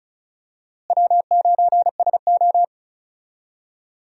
Morse code, heard as a keyed tone, about 36 wpm, 700 Hz. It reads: W9SO
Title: USA Call Signs - 4 Characters - Repeated 3x - 35wpm